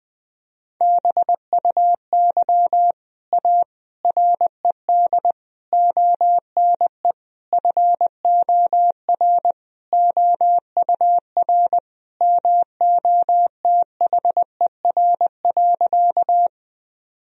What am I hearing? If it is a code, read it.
Morse code, heard as a keyed tone, about 21 wpm, 700 Hz. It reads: BUY A RED ONE FOR OUR MOTHER.
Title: Sentences from Top 1000 Words - Review Part 2 - 20wpm